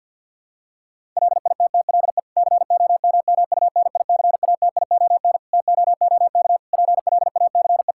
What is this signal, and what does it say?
Morse code, heard as a keyed tone, about 50 wpm, 700 Hz. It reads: LITTLE COMMUNICATION TOOK PLACE
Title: Sentences from Top 1000 Words - Review Part 6 - 50wpm